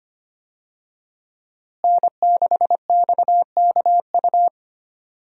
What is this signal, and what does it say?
Morse code, heard as a keyed tone, about 25 wpm, 700 Hz. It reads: N6XKU
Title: USA Call Signs - 5 Characters - Repeated 3x - 25wpm